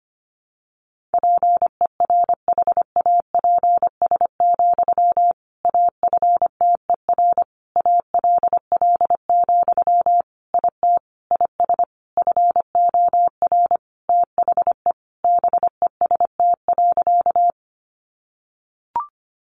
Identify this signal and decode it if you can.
Morse code, heard as a keyed tone, about 25 wpm, 700 Hz. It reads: PERHAPS, AFTER ALL, IT IS FOR THE BEST.
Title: Sentences from Top 500 Words 25wpm